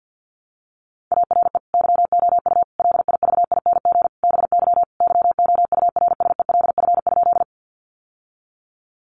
Morse code, heard as a keyed tone, about 50 wpm, 700 Hz. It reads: ARE YOU LIVING BY YOURSELF?
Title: Sentences from Top 1100 Words 50wpm